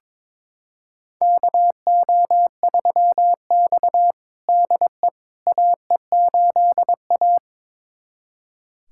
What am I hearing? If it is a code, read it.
Morse code, heard as a keyed tone, about 22 wpm, 700 Hz. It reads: KO3X DE AE8A
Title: USA Callsign DE Callsign - 4 Characters - 22wpm